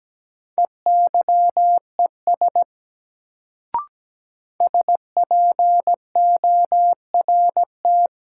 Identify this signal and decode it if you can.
Morse code, heard as a keyed tone, about 17 wpm, 700 Hz. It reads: EYES SPORT
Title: Sentences from Top 1000 Words - Review Part 4 - New Word Drill - 17wpm